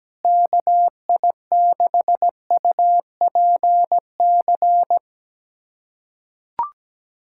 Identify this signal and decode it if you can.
Morse code, heard as a keyed tone, about 17 wpm, 700 Hz. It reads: KI6UPC
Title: USA Callsign DE Callsign 17wpm